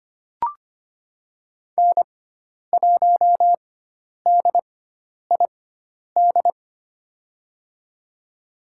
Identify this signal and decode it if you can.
Morse code, heard as a keyed tone, about 25 wpm, spaced as for 10 wpm, 700 Hz. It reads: N1DID
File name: Lesson 16g - Letter c - Call Signs 25wpm